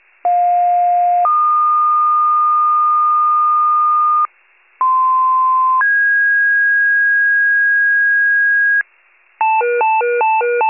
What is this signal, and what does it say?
Two-tone page, heard as a one-second tone, then a three-second tone, about 710 then 1220 Hz, about 1040 then 1650 Hz, with noise. It is heard from 0.2 to 8.8 s.